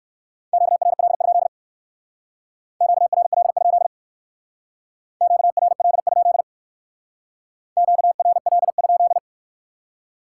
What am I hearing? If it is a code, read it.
Morse code, heard as a keyed tone, about 58 wpm, 700 Hz. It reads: QRL? QRL? QRL? QRL?